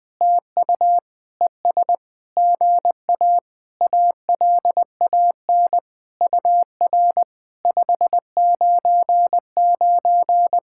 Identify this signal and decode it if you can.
Morse code, heard as a keyed tone, about 20 wpm, 700 Hz. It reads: TU ES GA ALAN UR 599
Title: POTA QSO Practice - 20wpm